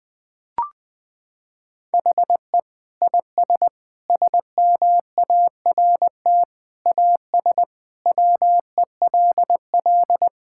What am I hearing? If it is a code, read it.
Morse code, heard as a keyed tone, about 20 wpm, 700 Hz. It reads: HE IS SMART AS WELL
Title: Sentences with 4-Word Phrases - 20wpm